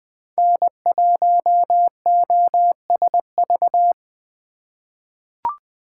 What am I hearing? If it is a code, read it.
Morse code, heard as a keyed tone, about 20 wpm, 700 Hz. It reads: N1OSV
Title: USA Call Signs - 5 Characters - 4x Character Spacing - 20wpm